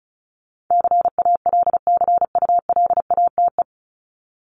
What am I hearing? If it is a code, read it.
Morse code, heard as a keyed tone, about 34 wpm, 700 Hz. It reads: CALCULATE
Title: Top 1751-2000 Words 35wpm